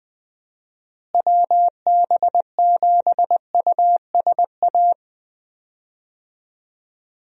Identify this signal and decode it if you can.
Morse code, heard as a keyed tone, about 20 wpm, 700 Hz. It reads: WB7USA
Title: USA Call Signs - 6 Characters - Repeated 3x - 20wpm